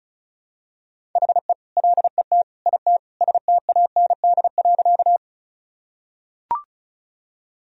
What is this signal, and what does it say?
Morse code, heard as a keyed tone, about 34 wpm, 700 Hz. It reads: HE LET IT STAND.
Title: Sentences from Top 1000 Words - Review Part 1 - 35wpm